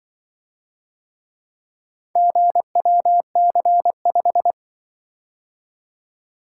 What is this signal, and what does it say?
Morse code, heard as a keyed tone, about 24 wpm, 700 Hz. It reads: GWC5
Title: Speed-Racing - Four Letter-Number Combinations - 20wpm